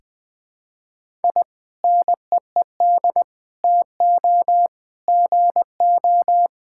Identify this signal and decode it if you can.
Morse code, heard as a keyed tone, about 20 wpm, 700 Hz. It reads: I NEED TO GO
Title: Sentences with 2-Word Phrases - 20wpm